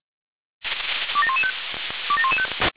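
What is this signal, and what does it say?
Five-tone selective call (ZVEI), heard as quick runs of five tones, tones of about 70 ms. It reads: 27115 27115
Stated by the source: SDR-received ZVEI five-tone selective call.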